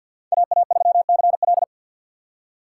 Morse code, heard as a keyed tone, about 46 wpm, 700 Hz. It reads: AA3CL
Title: USA Call Signs - 3x Character Spacing - 50wpm